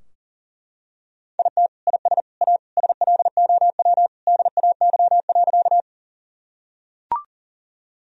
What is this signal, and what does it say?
Morse code, heard as a keyed tone, about 37 wpm, 700 Hz. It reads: IT IS A SLOW DAY.
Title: Sentences from Top 1000 Words - Review Part 2 - 40wpm